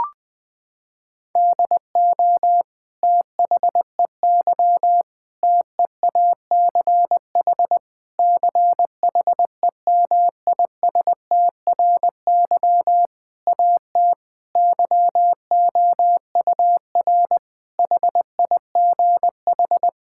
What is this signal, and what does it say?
Morse code, heard as a keyed tone, about 20 wpm, 700 Hz. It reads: DO THEY TEACH CHEMISTRY AT YOUR HIGH
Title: Sentences from Top 1000 Words - Review Part 6 - 20wpm